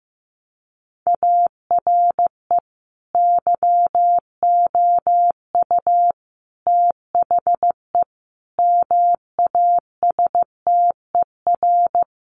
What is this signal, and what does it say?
Morse code, heard as a keyed tone, about 15 wpm, 700 Hz. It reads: ARE YOU THE MASTER
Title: Sentences from Top 1300 Words 15wpm